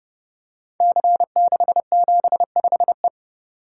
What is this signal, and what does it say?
Morse code, heard as a keyed tone, about 30 wpm, 700 Hz. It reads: C675E
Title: Code Groups - 5 Letter:Numbers - 30wpm